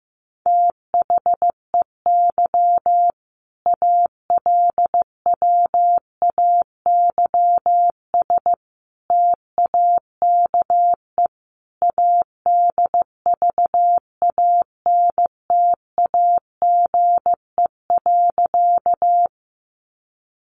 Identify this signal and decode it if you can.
Morse code, heard as a keyed tone, about 15 wpm, 700 Hz. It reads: THEY ALWAYS TAKE ADVANTAGE.
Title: Sentences from Top 900 Words 15wpm